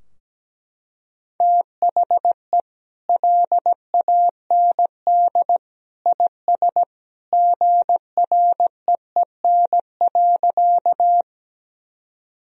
Morse code, heard as a keyed tone, about 17 wpm, 700 Hz. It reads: THE LAND IS GREEN.
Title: Sentences from Top 1000 Words - Review Part 2 - 17wpm